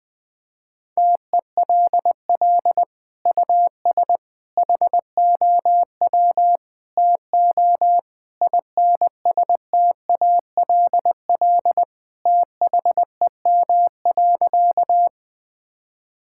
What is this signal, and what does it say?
Morse code, heard as a keyed tone, about 20 wpm, 700 Hz. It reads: TELL US HOW TO INSTALL THEM.
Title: Sentences from Top 1600 Words 20wpm